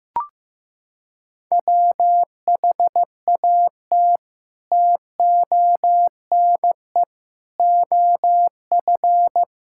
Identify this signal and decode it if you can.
Morse code, heard as a keyed tone, about 15 wpm, 700 Hz. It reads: WHAT TONE OF
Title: Sentences from Top 1600 Words 15wpm